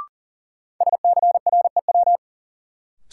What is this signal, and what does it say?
Morse code, heard as a keyed tone, about 40 wpm, 700 Hz. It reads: SCREW
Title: Sentences from Top 1600 Words - New Word Drill - 40wpm